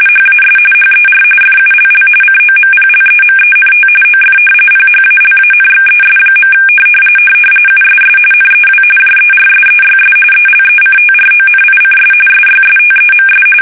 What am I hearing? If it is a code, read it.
RTTY, about 45.45 baud. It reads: HAS HER FULL COMPLEMENT OF PHALANXES. USS NEW JERSEYS ELECTRONICS SYSTEMS
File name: fsk850 / battleshipnj